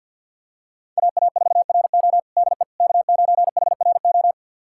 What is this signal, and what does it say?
Morse code, heard as a keyed tone, about 47 wpm, 700 Hz. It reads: AA4RO DE K9HRO
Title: USA Callsign DE Callsign - 5 Characters - 50wpm